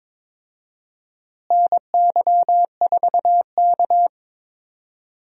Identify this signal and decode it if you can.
Morse code, heard as a keyed tone, about 21 wpm, 700 Hz. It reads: NY4K
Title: USA Call Signs - 4 Characters - Repeated 2x - 22wpm